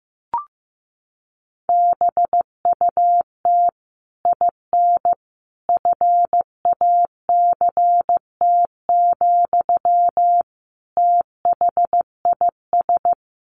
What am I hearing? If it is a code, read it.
Morse code, heard as a keyed tone, about 15 wpm, 700 Hz. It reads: BUT IN FACT, THIS
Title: Sentences from Top 1500 Words 15wpm